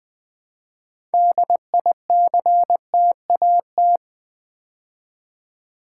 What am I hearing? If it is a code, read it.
Morse code, heard as a keyed tone, about 20 wpm, 700 Hz. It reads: DICTAT
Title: Build A Word - 2 Letters At A Time - 8-Letter Words - 20wpm